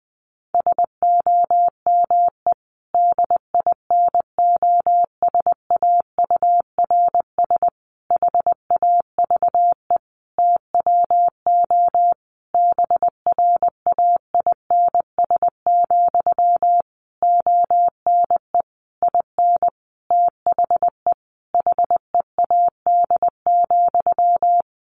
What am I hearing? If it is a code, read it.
Morse code, heard as a keyed tone, about 20 wpm, 700 Hz. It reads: SOME DINOSAURS HAVE TWO BRAINS, ONE IN THE HEAD,
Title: A Sound Of Thunder 20wpm